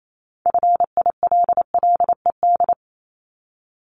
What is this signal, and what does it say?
Morse code, heard as a keyed tone, about 28 wpm, 700 Hz. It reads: FILLED
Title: Multi-letter Phonemes 28wpm